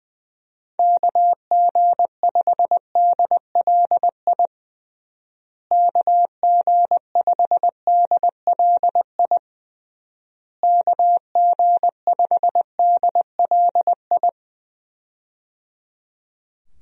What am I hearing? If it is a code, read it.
Morse code, heard as a keyed tone, about 20 wpm, 700 Hz. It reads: KG5DLI KG5DLI KG5DLI
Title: USA Call Signs - 6 Characters - Repeated 3x - 20wpm